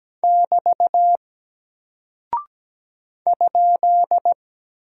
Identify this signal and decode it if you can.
Morse code, heard as a keyed tone, about 17 wpm, 700 Hz. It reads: <BT> ?
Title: Single Character (Letter:Number:Punctuation:Prosign) 17wpm